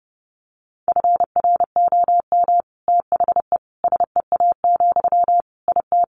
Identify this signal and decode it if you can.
Morse code, heard as a keyed tone, about 30 wpm, 700 Hz. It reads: FROM THE SEA, IT
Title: Sentences from Top 1500 Words 30wpm